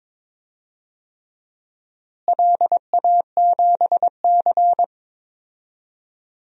Morse code, heard as a keyed tone, about 22 wpm, 700 Hz. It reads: LA7C
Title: DXCC-100 Call Signs - 4 Characters - Repeated 3x - 22wpm